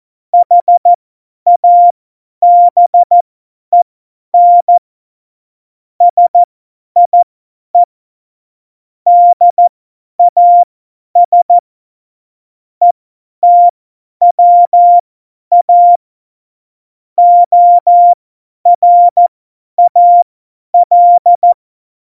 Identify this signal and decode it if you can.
Morse code, heard as a keyed tone, about 14 wpm, spaced as for 10 wpm, 700 Hz. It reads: HABEN SIE DAS ETWA ORAL
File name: Morsekurs / Witze / Witz 22. September 2020, 50 F70